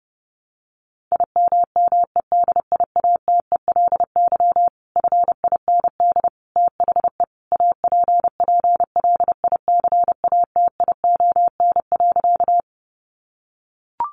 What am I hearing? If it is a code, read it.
Morse code, heard as a keyed tone, about 30 wpm, 700 Hz. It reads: IMMEDIATELY FIND THE APPLICATION.
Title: Sentences from Top 1200 Words 30wpm